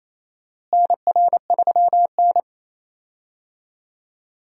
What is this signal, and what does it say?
Morse code, heard as a keyed tone, about 28 wpm, 700 Hz. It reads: NR3N
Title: USA Call Signs - 4 Characters - Repeated 2x - 28wpm